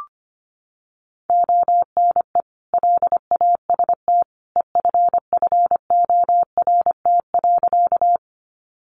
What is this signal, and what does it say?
Morse code, heard as a keyed tone, about 25 wpm, 700 Hz. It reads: ONE LAST EFFORT.
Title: Sentences from Top 1200 Words 25wpm